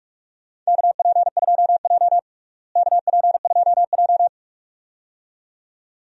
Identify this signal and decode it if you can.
Morse code, heard as a keyed tone, about 45 wpm, 700 Hz. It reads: KP2J KP2J
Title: DXCC-100 Call Signs - 4 Characters - Repeated 2x - 45wpm